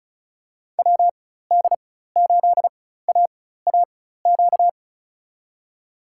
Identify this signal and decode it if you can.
Morse code, heard as a keyed tone, about 36 wpm, spaced as for 16 wpm, 700 Hz. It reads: WD8AAQ
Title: USA Call Signs - 6 Characters - 4x Character Spacing - 35wpm